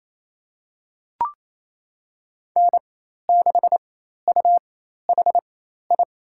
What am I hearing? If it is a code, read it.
Morse code, heard as a keyed tone, about 28 wpm, spaced as for 13 wpm, 700 Hz. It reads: N6UHI
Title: USA Call Signs - 5 Characters - 4x Character Spacing - 28wpm